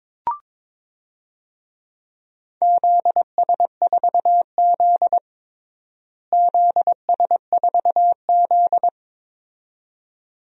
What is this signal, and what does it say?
Morse code, heard as a keyed tone, about 22 wpm, 700 Hz. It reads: ZS4Z ZS4Z
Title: DXCC-100 Call Signs - 4 Characters - Repeated 2x - 22wpm